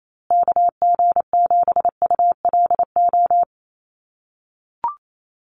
Morse code, heard as a keyed tone, about 28 wpm, 700 Hz. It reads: KG7ULO
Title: USA Call Signs Repeated 3x 28wpm